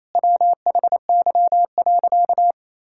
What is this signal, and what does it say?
Morse code, heard as a keyed tone, about 28 wpm, 700 Hz. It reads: WHY.
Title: Sentences from Top 1600 Words 28wpm